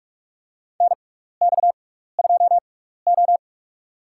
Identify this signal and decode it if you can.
Morse code, heard as a keyed tone, about 46 wpm, spaced as for 16 wpm, 700 Hz. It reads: NX2O
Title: USA Call Signs - 4 Characters - 6x Character Spacing - 45wpm